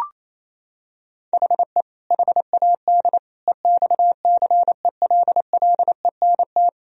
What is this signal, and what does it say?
Morse code, heard as a keyed tone, about 28 wpm, 700 Hz. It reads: HE HAD EXCELLENT
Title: Sentences from Top 1000 Words - Review Part 3 - 28wpm